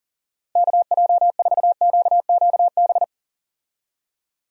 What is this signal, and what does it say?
Morse code, heard as a keyed tone, about 40 wpm, 700 Hz. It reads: KJ4QQB
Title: USA Call Signs - 6 Characters - 40wpm